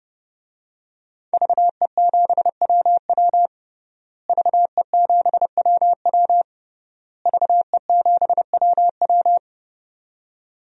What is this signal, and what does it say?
Morse code, heard as a keyed tone, about 28 wpm, 700 Hz. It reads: VE7WW VE7WW VE7WW
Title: DXCC-100 Call Signs - 5 Characters - Repeated 3x - 30wpm